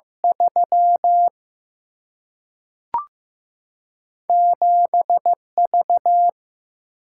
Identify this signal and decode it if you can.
Morse code, heard as a keyed tone, about 15 wpm, 700 Hz. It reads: 3 7 V
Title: Single-Letter Single-Number 15wpm